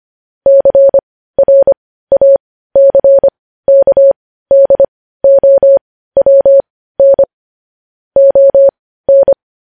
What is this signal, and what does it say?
Morse code, heard as a keyed tone, about 25 wpm, spaced as for 15 wpm, 550 Hz. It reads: CRACKDOWN ON
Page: news headlines in morse code at 15 wpm